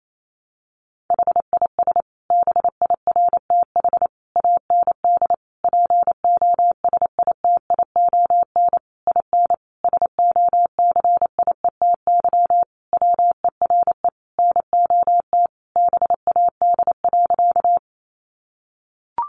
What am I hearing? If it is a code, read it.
Morse code, heard as a keyed tone, about 28 wpm, 700 Hz. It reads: HIS BIRTH AND POSITION IN SOCIETY WERE NOT BAD.
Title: Sentences from Top 900 Words 28wpm